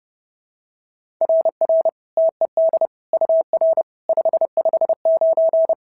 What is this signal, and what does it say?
Morse code, heard as a keyed tone, about 32 wpm, 690 Hz. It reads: RR TED UR 559
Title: SOTA QSO Practice 30wpm